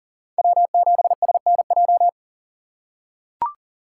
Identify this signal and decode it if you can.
Morse code, heard as a keyed tone, about 40 wpm, 700 Hz. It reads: W7SNJ
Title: USA Call Signs - 5 Characters - 6x Character Spacing - 40wpm